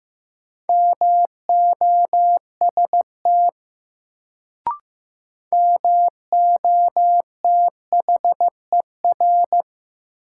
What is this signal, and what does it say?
Morse code, heard as a keyed tone, about 16 wpm, 700 Hz. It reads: MOST MOTHER
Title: Sentences from Top 200 Words - New Word Drill - 15wpm